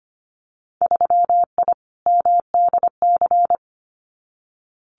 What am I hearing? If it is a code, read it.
Morse code, heard as a keyed tone, about 25 wpm, 690 Hz. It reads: 3I MDC
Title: Winter Field Day Exchanges Encore 25wpm